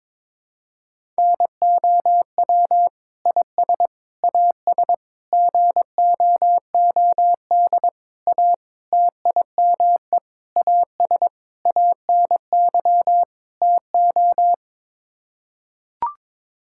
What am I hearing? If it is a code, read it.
Morse code, heard as a keyed tone, about 22 wpm, 700 Hz. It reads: NOW IS AS GOOD A TIME AS ANY TO
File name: Sets of 9 Words - 22wpm